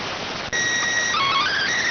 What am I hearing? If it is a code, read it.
SSTV (slow-scan television), PD 290.